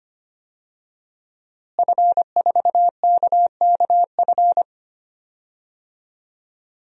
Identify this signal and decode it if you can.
Morse code, heard as a keyed tone, about 25 wpm, 700 Hz. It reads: F4KKF
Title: DXCC-100 Call Signs - 5 Characters - Repeated 2x - 25wpm